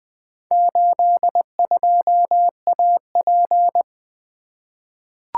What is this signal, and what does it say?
Morse code, heard as a keyed tone, about 20 wpm, 700 Hz. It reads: 82AP
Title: Four Letter-Number Combinations - 20wpm